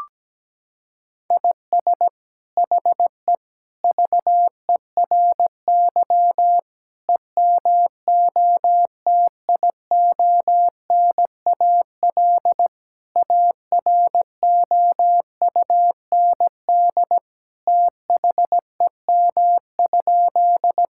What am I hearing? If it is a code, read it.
Morse code, heard as a keyed tone, about 17 wpm, 700 Hz. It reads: IS HE VERY EMOTIONAL AROUND THEM?
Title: Sentences from Top 1000 Words - Review Part 5 - 17wpm